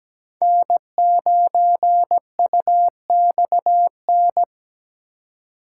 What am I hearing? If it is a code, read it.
Morse code, heard as a keyed tone, about 17 wpm, 700 Hz. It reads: N9UXN
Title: USA Call Signs - 5 Characters - 2x Character Spacing - 17wpm